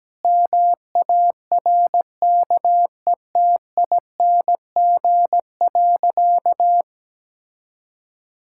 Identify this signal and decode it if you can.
Morse code, heard as a keyed tone, about 17 wpm, 700 Hz. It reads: MARKETING.
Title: Sentences from Top 1000 Words - Review Part 6 - 17wpm